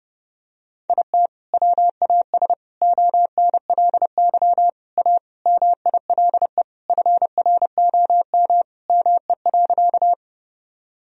Morse code, heard as a keyed tone, about 30 wpm, 700 Hz. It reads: IT WAS ONLY A MILE FROM ME.
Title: Sentences from Top 1000 Words - Review Part 1 - 30wpm